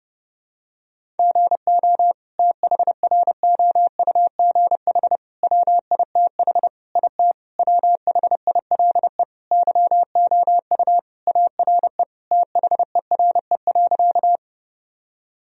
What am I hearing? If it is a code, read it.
Morse code, heard as a keyed tone, about 30 wpm, 700 Hz. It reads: GO THROUGH WITH IT WHILE YOU ARE THERE.
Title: Sentences from Top 1000 Words - Review Part 1 - 30wpm